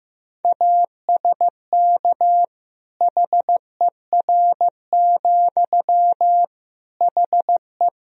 Morse code, heard as a keyed tone, about 16 wpm, 700 Hz. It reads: ASK HER, HE
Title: Sentences from Top 200 Words and Abbreviations 15wpm